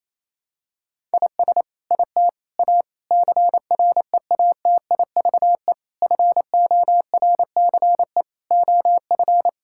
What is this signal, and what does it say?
Morse code, heard as a keyed tone, about 28 wpm, 700 Hz. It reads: IS IT A CREATIVE FORCE OF
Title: Sentences from Top 1000 Words - Review Part 5 - 28wpm